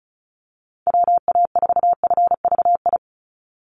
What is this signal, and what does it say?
Morse code, heard as a keyed tone, about 34 wpm, 700 Hz. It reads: WA4FVI
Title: USA Call Signs 35wpm